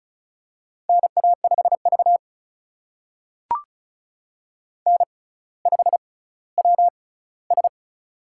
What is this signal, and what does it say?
Morse code, heard as a keyed tone, about 36 wpm, spaced as for 18 wpm, 700 Hz. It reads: NA5V N5WS
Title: USA Call Signs - 4 Characters - 6x Character Spacing - 35wpm